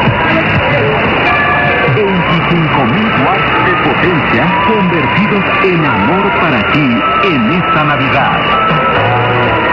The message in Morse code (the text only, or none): none